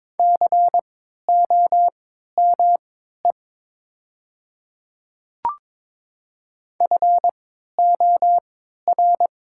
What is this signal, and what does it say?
Morse code, heard as a keyed tone, about 22 wpm, spaced as for 13 wpm, 700 Hz. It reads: COME FOR
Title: 3x Character Spacing - Top 100 Words - 22wpm